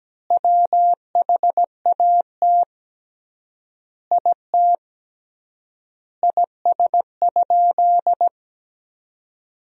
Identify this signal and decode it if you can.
Morse code, heard as a keyed tone, about 17 wpm, 700 Hz. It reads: WHAT IT IS?
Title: 3x Word-Spacing - Sentences from Top 100 Words - 17wpm